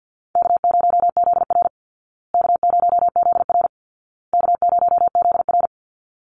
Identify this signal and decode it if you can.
Morse code, heard as a keyed tone, about 50 wpm, 700 Hz. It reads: K0ZR K0ZR K0ZR
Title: Easy Call Signs Repeated 3x - 50wpm